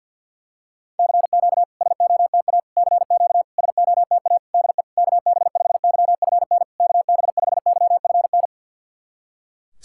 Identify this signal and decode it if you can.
Morse code, heard as a keyed tone, about 46 wpm, 690 Hz. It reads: CQ SOTA CQ SOTA DE KB5YFN KB5YFN
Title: SOTA QSO Practice 50wpm